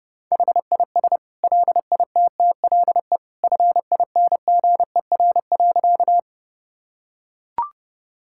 Morse code, heard as a keyed tone, about 30 wpm, 700 Hz. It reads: HIS LITTLE FINGER.
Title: Sentences from Top 1600 Words 30wpm